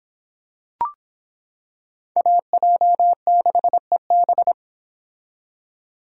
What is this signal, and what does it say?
Morse code, heard as a keyed tone, about 26 wpm, 700 Hz. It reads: AJ6EB